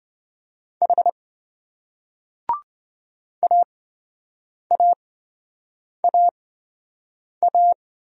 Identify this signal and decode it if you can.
Morse code, heard as a keyed tone, about 26 wpm, spaced as for 7 wpm, 700 Hz. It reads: H AAAA